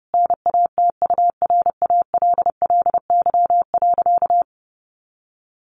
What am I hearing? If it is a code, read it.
Morse code, heard as a keyed tone, about 28 wpm, 700 Hz. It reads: NATURALLY.
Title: Sentences from Top 1300 Words 30wpm